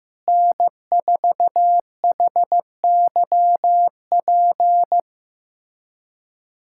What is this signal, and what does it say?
Morse code, heard as a keyed tone, about 16 wpm, 700 Hz. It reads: N4HYP
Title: USA Callsign DE Callsign - 5 Characters - 15wpm